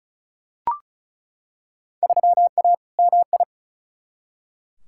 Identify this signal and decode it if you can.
Morse code, heard as a keyed tone, about 34 wpm, 700 Hz. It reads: 3A MI
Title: Field Day Exchanges - 35wpm